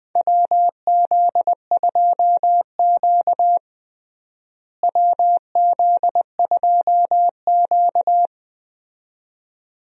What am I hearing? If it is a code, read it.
Morse code, heard as a keyed tone, about 20 wpm, 700 Hz. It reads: WZ2Q WZ2Q
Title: USA Call Signs - 4 Characters - Repeated 2x - 20wpm